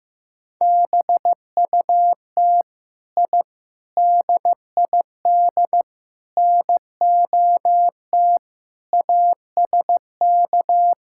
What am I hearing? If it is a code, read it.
Morse code, heard as a keyed tone, about 15 wpm, 700 Hz. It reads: BUT I DID NOT ASK
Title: Sentences from Top 1000 Words - Review Part 5 - 15wpm